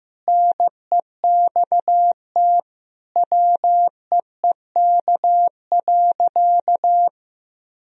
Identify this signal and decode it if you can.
Morse code, heard as a keyed tone, about 15 wpm, 700 Hz. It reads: NEXT WEEK.
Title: Sentences from Top 1000 Words - Review Part 2 - 15wpm